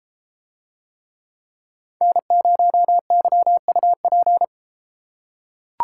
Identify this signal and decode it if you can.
Morse code, heard as a keyed tone, about 32 wpm, 700 Hz. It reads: N0YUP